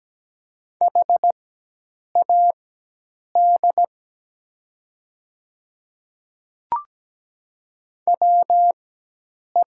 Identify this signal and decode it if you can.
Morse code, heard as a keyed tone, about 17 wpm, spaced as for 8 wpm, 700 Hz. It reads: HAD WE
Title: 4x Character Spacing - Top 100 Words - 17wpm